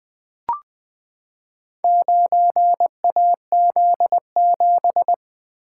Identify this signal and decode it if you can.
Morse code, heard as a keyed tone, about 21 wpm, 700 Hz. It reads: 9AZ7
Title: Four Letter-Number Combinations - 20wpm